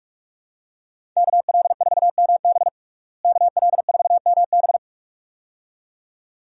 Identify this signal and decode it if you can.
Morse code, heard as a keyed tone, about 45 wpm, 700 Hz. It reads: KL4MB KL4MB
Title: USA Call Signs - 5 Characters - Repeated 2x - 45wpm